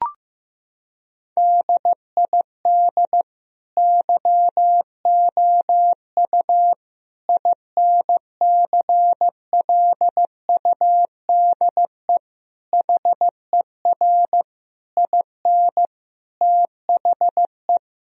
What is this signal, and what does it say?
Morse code, heard as a keyed tone, about 15 wpm, 700 Hz. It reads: DID YOU INCLUDE HER IN THE
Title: Sentences from Top 1000 Words - Review Part 3 - 15wpm